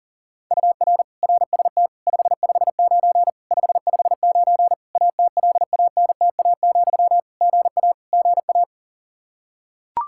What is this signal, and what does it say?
Morse code, heard as a keyed tone, about 40 wpm, 700 Hz. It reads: UR RST 559 559 ATLANTA, GA GA
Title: K1USN SST QSO Practice 40wpm